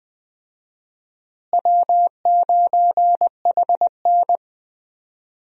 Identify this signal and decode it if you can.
Morse code, heard as a keyed tone, about 20 wpm, 690 Hz. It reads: W9HN
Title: USA Call Signs - 4 Characters - Repeated 3x - 20wpm